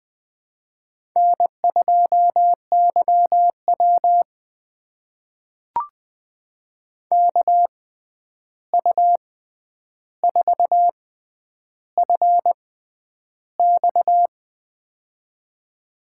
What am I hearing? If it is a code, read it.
Morse code, heard as a keyed tone, about 20 wpm, spaced as for 10 wpm, 700 Hz. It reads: N2YW KU4FX
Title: USA Call Signs - 6x Character Spacing - 20wpm